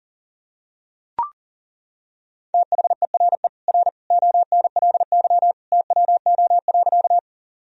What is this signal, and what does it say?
Morse code, heard as a keyed tone, about 40 wpm, 700 Hz. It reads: THERE R ONLY TWO.
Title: Sentences from Top 200 Words and Abbreviations 40wpm